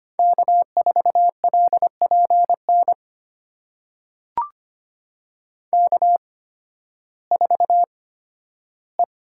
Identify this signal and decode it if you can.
Morse code, heard as a keyed tone, about 25 wpm, spaced as for 13 wpm, 700 Hz. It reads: K4LPN K4E
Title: USA Call Signs - 5 Characters - 8x Character Spacing - 25wpm